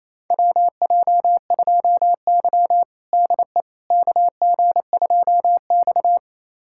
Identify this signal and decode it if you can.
Morse code, heard as a keyed tone, about 28 wpm, 700 Hz. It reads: WJ2Y DE KG2X
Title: USA Callsign DE Callsign - 4 Characters - 28wpm